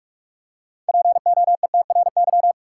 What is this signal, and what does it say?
Morse code, heard as a keyed tone, about 44 wpm, 700 Hz. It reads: POETRY
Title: Six Letter Words - 45wpm